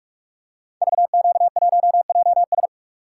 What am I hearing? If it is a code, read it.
Morse code, heard as a keyed tone, about 44 wpm, 700 Hz. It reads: VQ1JS
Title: Code Groups - 5 Letter:Numbers - 45wpm